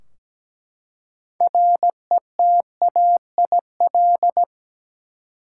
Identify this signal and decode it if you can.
Morse code, heard as a keyed tone, about 17 wpm, 700 Hz. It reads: RETAIL
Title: Practice QSO - Exchange 3 - More Info - 17wpm